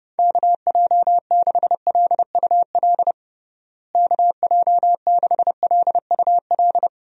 Morse code, heard as a keyed tone, about 32 wpm, 700 Hz. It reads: KJ6LUL KJ6LUL
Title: USA Call Signs - 6 Characters - Repeated 2x - 30wpm